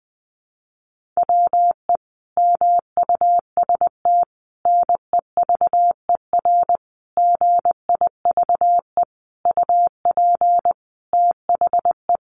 Morse code, heard as a keyed tone, about 19 wpm, 700 Hz. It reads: WE MUST NEVER GIVE UP THE
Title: Sentences from Top 1500 Words 20wpm